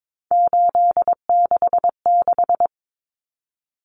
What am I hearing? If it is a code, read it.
Morse code, heard as a keyed tone, about 22 wpm, 700 Hz. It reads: 866
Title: Common Cut Numbers - 3-Digit 22wpm